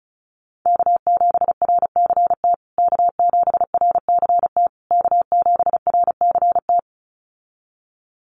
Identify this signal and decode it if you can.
Morse code, heard as a keyed tone, about 34 wpm, 700 Hz. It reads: K7RCT K7RCT K7RCT
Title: USA Call Signs Repeated 3x 35wpm